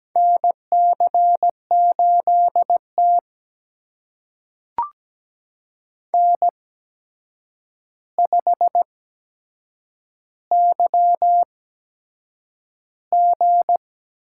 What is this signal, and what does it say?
Morse code, heard as a keyed tone, about 17 wpm, spaced as for 7 wpm, 700 Hz. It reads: NC8T N5YG
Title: USA Call Signs - 4 Characters - 8x Character Spacing - 17wpm